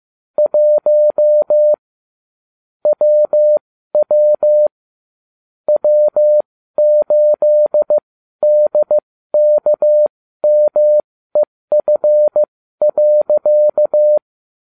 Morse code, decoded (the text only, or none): 1 WW W8DKMEF.